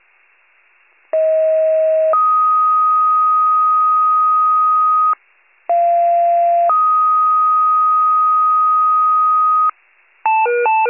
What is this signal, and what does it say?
Two-tone page, heard as a one-second tone, then a three-second tone, about 640 then 1250 Hz, about 690 then 1250 Hz. It is heard from 1.1 to 9.7 s.
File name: wcalert